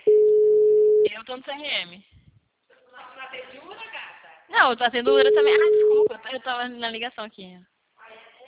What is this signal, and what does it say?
Ringback tone (Europe, 425 Hz), with speech.
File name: out- -2012